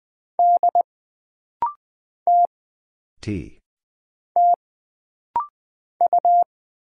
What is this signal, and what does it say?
Morse code, heard as a keyed tone, about 20 wpm, 700 Hz. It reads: D T T U
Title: Single Letter-Number - Rapid-Fire 20wpm